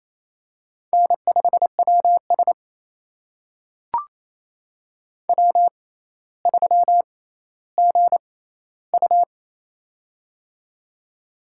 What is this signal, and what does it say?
Morse code, heard as a keyed tone, about 28 wpm, spaced as for 14 wpm, 700 Hz. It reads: N5WS W3GU
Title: USA Call Signs - 4 Characters - 6x Character Spacing - 28wpm